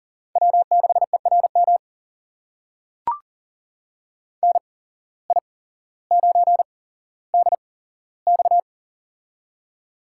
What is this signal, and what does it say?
Morse code, heard as a keyed tone, about 42 wpm, spaced as for 17 wpm, 700 Hz. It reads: W6ERM NI9DX
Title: USA Call Signs - 8x Character Spacing - 40wpm